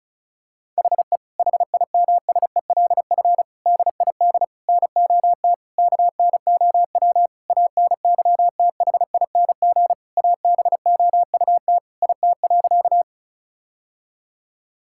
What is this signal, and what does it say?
Morse code, heard as a keyed tone, about 36 wpm, 700 Hz. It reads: HE HIMSELF DID NOT KNOW ANYTHING ABOUT IT.
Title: Sentences from Top 1600 Words 35wpm